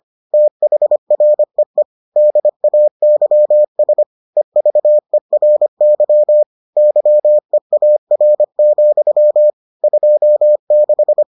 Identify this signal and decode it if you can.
Morse code, heard as a keyed tone, about 25 wpm, 600 Hz. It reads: THREE DAYS EVERY YEAR, 26